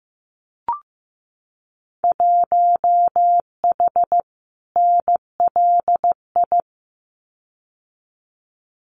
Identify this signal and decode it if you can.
Morse code, heard as a keyed tone, about 15 wpm, 700 Hz. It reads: 1H NLI
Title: Winter Field Day Exchanges 15wpm